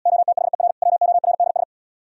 Morse code, heard as a keyed tone, about 77 wpm, 700 Hz. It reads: OTHER WORDS